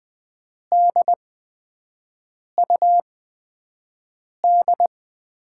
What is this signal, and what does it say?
Morse code, heard as a keyed tone, about 20 wpm, spaced as for 5 wpm, 700 Hz. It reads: DUD
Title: USA Call Signs - 5 Characters - 8x Character Spacing - 20wpm